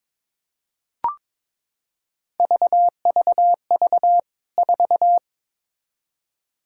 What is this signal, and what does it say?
Morse code, heard as a keyed tone, about 22 wpm, 700 Hz. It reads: VVV 4
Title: K1USN SST QSO Practice 22wpm